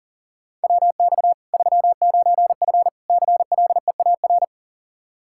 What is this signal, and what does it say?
Morse code, heard as a keyed tone, about 40 wpm, 700 Hz. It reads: WX 39F CLEAR
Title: Practice QSO - Exchange 2 - Info - 40wpm